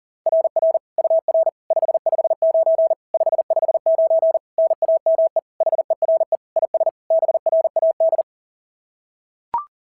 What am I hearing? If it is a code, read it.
Morse code, heard as a keyed tone, about 40 wpm, 690 Hz. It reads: RR UR 559 559 NAME HERE IS BRAD
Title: SOTA QSO Practice 40wpm